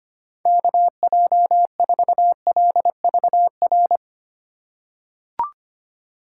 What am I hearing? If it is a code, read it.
Morse code, heard as a keyed tone, about 25 wpm, 700 Hz. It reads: KJ4LVR
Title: USA Call Signs - 6 Characters - 4x Character Spacing - 25wpm